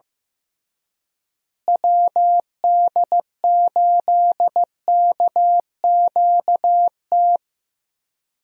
Morse code, heard as a keyed tone, about 15 wpm, 700 Hz. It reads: WD8KQT
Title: USA Call Signs - 6 Characters - Repeated 3x - 15wpm